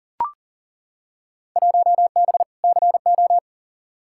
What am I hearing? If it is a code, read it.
Morse code, heard as a keyed tone, about 40 wpm, 700 Hz. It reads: 1B CO
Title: Field Day Exchanges - 40wpm